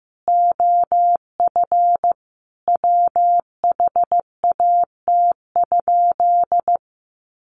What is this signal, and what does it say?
Morse code, heard as a keyed tone, about 15 wpm, 700 Hz. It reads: OF WHAT?
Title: Sentences from Top 1500 Words 15wpm